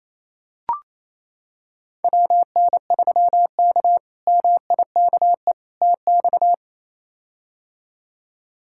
Morse code, heard as a keyed tone, about 28 wpm, 700 Hz. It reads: WN3K MIKE TX
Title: K1USN SST QSO Practice - 28wpm